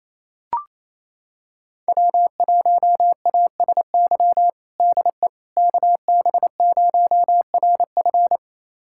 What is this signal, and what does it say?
Morse code, heard as a keyed tone, about 28 wpm, 700 Hz. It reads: W1ASY DE KB0RF
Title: USA Callsign DE Callsign - 5 Characters - 28wpm